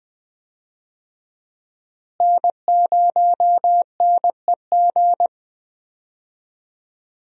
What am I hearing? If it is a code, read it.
Morse code, heard as a keyed tone, about 21 wpm, 700 Hz. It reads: N0NEG
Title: USA Call Signs - 5 Characters - Repeated 3x - 20wpm